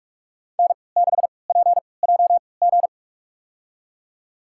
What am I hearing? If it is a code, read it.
Morse code, heard as a keyed tone, about 46 wpm, spaced as for 25 wpm, 700 Hz. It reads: N6PJG
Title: USA Call Signs - 5 Characters - 3x Character Spacing - 45wpm